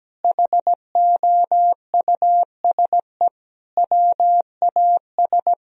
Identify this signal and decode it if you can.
Morse code, heard as a keyed tone, about 17 wpm, 700 Hz. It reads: HOUSE WAS
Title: Sentences from Top 1000 Words - Review Part 2 - 17wpm